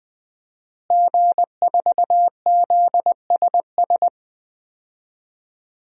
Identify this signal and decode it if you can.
Morse code, heard as a keyed tone, about 20 wpm, 700 Hz. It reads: G4ZSS
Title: DXCC-100 Call Signs - 5 Characters - Repeated 2x - 20wpm